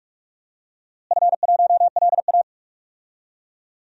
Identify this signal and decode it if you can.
Morse code, heard as a keyed tone, about 44 wpm, 700 Hz. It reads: F1LA
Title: DXCC-100 Call Signs - 4 Characters - Repeated 3x - 45wpm